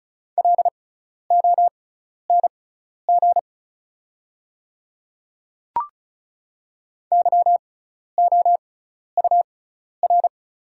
Morse code, heard as a keyed tone, about 34 wpm, spaced as for 12 wpm, 700 Hz. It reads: LONG YOUR
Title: 6x Character Spacing - Top 100 Words - 35wpm